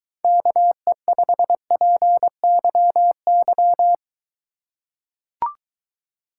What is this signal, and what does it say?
Morse code, heard as a keyed tone, about 23 wpm, 700 Hz. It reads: KE5PYY